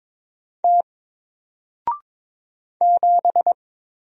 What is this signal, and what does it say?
Morse code, heard as a keyed tone, about 22 wpm, 700 Hz. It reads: T 7
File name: Single-Letter-Number-22wpm